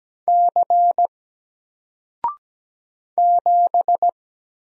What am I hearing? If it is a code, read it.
Morse code, heard as a keyed tone, about 17 wpm, 700 Hz. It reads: C 7
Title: Single Character (Letter:Number:Punctuation:Prosign) Alt 17wpm